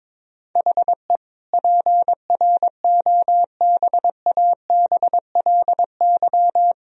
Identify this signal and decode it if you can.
Morse code, heard as a keyed tone, about 22 wpm, 700 Hz. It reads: HE PROBABLY
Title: Sentences from Top 1000 Words - Review Part 4 - 22wpm